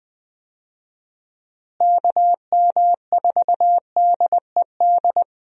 Morse code, heard as a keyed tone, about 21 wpm, 700 Hz. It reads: KM4DED
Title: USA Call Signs - 6 Characters - Repeated 3x - 20wpm